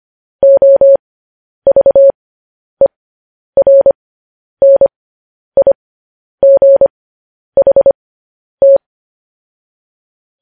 Morse code, heard as a keyed tone, about 25 wpm, spaced as for 10 wpm, 550 Hz. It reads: OVERNIGHT
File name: podcast / News headlines-1x-25x10